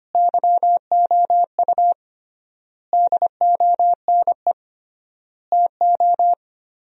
Morse code, heard as a keyed tone, about 25 wpm, 700 Hz. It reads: YOU DONE TO
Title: 3x Word-Spacing - Sentences from Top 500 Words - 25wpm